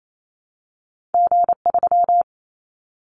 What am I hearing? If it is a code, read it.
Morse code, heard as a keyed tone, about 28 wpm, 700 Hz. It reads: G3
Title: Callsign Prefixes for Top 25 DXCC - Repeated 3x - 28wpm